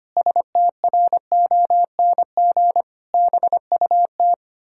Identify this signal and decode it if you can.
Morse code, heard as a keyed tone, about 25 wpm, 700 Hz. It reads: STRONG BUT
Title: Pangram Challenge v2 - 25wpm